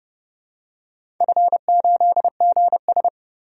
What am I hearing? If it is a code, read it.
Morse code, heard as a keyed tone, about 30 wpm, 700 Hz. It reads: F8GS
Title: DXCC-100 Call Signs - 4 Characters - 30wpm